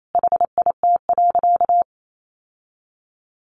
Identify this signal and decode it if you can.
Morse code, heard as a keyed tone, about 28 wpm, 700 Hz. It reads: HIT.
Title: Sentences from Top 1200 Words 28wpm